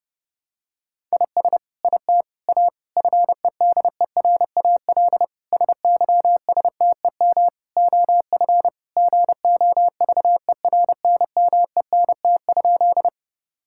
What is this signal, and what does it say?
Morse code, heard as a keyed tone, about 30 wpm, 700 Hz. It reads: IS IT A FEDERAL SYSTEM OF GOVERNMENT?
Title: Sentences from Top 1000 Words - Review Part 5 - 30wpm